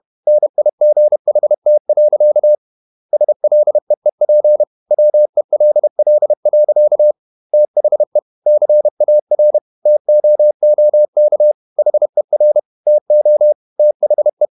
Morse code, heard as a keyed tone, about 31 wpm, 600 Hz. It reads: NIGHT. SLEEP WELL. THE CAR TOOK HER TO THE
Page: Never 31wpm | CW med Gnister